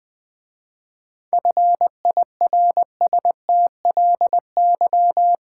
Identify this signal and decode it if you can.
Morse code, heard as a keyed tone, about 20 wpm, 700 Hz. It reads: FIRSTLY
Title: 7 Letter Words - First 3 Letters Given - 20wpm